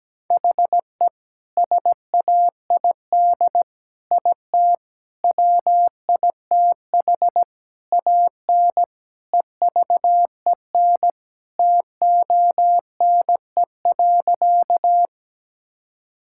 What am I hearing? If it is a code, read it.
Morse code, heard as a keyed tone, about 17 wpm, 700 Hz. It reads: HE SAID IT WITH AN EVEN TONE.
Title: Sentences from Top 1600 Words 17wpm